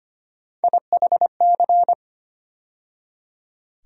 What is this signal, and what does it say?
Morse code, heard as a keyed tone, about 25 wpm, 700 Hz. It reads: IHC
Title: USA Call Sign Postfixes - 3 Letters 25wpm